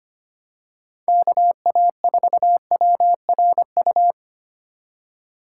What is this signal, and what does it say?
Morse code, heard as a keyed tone, about 25 wpm, 700 Hz. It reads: KA4WRU
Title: USA Call Signs - 6 Characters - 3x Character Spacing - 25wpm